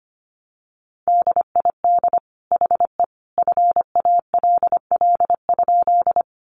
Morse code, heard as a keyed tone, about 25 wpm, 700 Hz. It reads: DID HE FALL?
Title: Sentences from Top 500 Words 25wpm